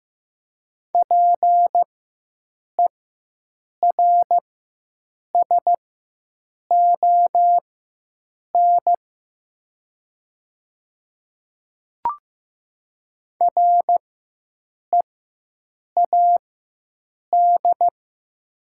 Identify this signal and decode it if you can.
Morse code, heard as a keyed tone, about 15 wpm, spaced as for 7 wpm, 700 Hz. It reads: PERSON READ
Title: 4x Character Spacing - Top 500 Words - 15wpm